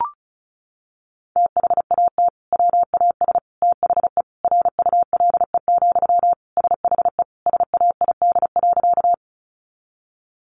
Morse code, heard as a keyed tone, about 34 wpm, 700 Hz. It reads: THAT WAS THE RULE, SHE SAID.
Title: Sentences from Top 400 Words 35wpm